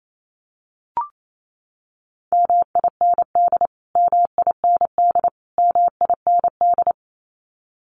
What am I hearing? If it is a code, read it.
Morse code, heard as a keyed tone, about 28 wpm, 700 Hz. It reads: MIND MIND MIND
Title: Top 500 Words Repeated 28wpm